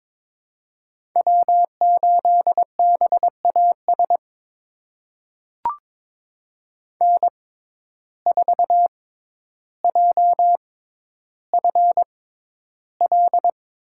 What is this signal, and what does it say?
Morse code, heard as a keyed tone, about 22 wpm, spaced as for 11 wpm, 700 Hz. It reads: W8BAS N4JFL
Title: USA Call Signs - 5 Characters - 6x Character Spacing - 22wpm